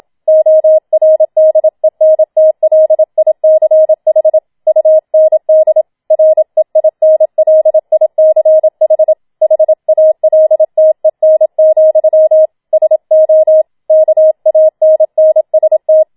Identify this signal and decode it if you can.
Morse code, heard as a keyed tone, about 27 wpm, 630 Hz. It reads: ORDENTLICH UND REINLICH HALTEN, SO KANNST
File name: Schneewittchen Teil2 27wpm